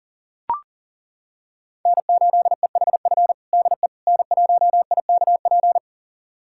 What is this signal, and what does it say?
Morse code, heard as a keyed tone, about 40 wpm, 690 Hz. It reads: N8EHF DE N1IKP
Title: USA Callsign DE Callsign - 5 Characters - 40wpm